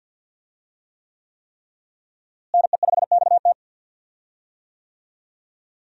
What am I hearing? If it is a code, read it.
Morse code, heard as a keyed tone, about 52 wpm, 700 Hz. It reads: NE5XT
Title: USA Call Signs - 5 Characters - Repeated 3x - 50wpm